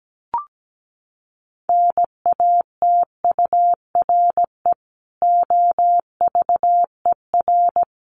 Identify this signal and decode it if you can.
Morse code, heard as a keyed tone, about 17 wpm, 700 Hz. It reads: NATURE OVER
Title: Sentences from Top 1500 Words 17wpm